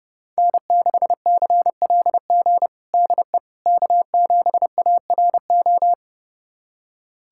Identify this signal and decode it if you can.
Morse code, heard as a keyed tone, about 30 wpm, 700 Hz. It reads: N6CLG DE K7ARO
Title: USA Callsign DE Callsign - 5 Characters - 30wpm